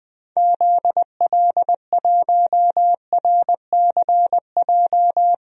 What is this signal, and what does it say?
Morse code, heard as a keyed tone, about 20 wpm, 700 Hz. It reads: ZL1RCJ
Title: DXCC-100 Call Signs - 6 Characters - Repeated 2x - 20wpm